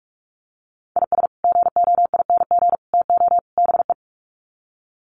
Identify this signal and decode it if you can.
Morse code, heard as a keyed tone, about 44 wpm, 700 Hz. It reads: IS GOING TO BE
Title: Sets of 4 Words 45wpm